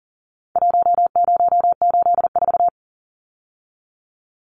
Morse code, heard as a keyed tone, about 40 wpm, 700 Hz. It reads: 1084
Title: Numbers - Contest Style - 4-Digit 40wpm